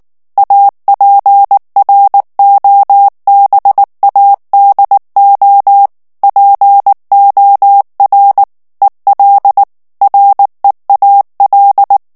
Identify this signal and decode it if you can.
Morse code, heard as a keyed tone, about 19 wpm, 800 Hz. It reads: APROBADO POR EL REAL
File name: morse mp3 / ESPAGNOL TEXTE EN CLAIR 2 VITESSE 18 MOTS MINUTES